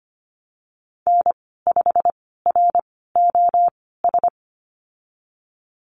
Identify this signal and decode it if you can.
Morse code, heard as a keyed tone, about 25 wpm, spaced as for 16 wpm, 700 Hz. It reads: N5ROS
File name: Lesson 15g - Number 5 - Call Signs 25wpm